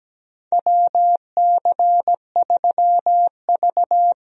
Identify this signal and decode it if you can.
Morse code, heard as a keyed tone, about 17 wpm, 700 Hz. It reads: WC3V
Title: USA Call Signs - 4 Characters - Repeated 3x - 17wpm